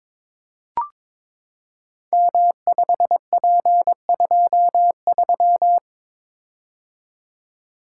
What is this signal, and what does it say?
Morse code, heard as a keyed tone, about 22 wpm, 700 Hz. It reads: M5P23
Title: Code Groups - 5 Letter:Numbers - 22wpm